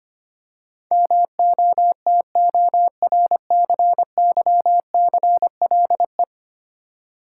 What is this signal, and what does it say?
Morse code, heard as a keyed tone, about 25 wpm, 700 Hz. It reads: MOTORCYCLE
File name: Ten Letter Words - 25wpm